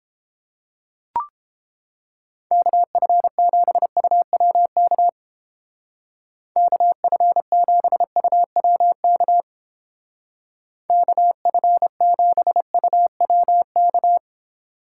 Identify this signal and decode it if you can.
Morse code, heard as a keyed tone, about 29 wpm, 700 Hz. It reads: KF7UWK KF7UWK KF7UWK